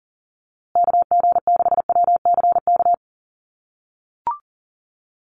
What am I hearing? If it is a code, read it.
Morse code, heard as a keyed tone, about 40 wpm, 700 Hz. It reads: KG6WCK
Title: USA Call Signs Repeated 3x 40wpm